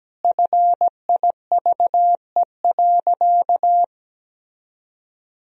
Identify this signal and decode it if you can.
Morse code, heard as a keyed tone, about 17 wpm, 700 Hz. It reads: FIVE.
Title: Sentences from Top 1000 Words - Review Part 2 - 17wpm